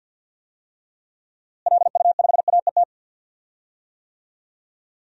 Morse code, heard as a keyed tone, about 51 wpm, 700 Hz. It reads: LU5AET
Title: DXCC-100 Call Signs - 6 Characters - Repeated 2x - 50wpm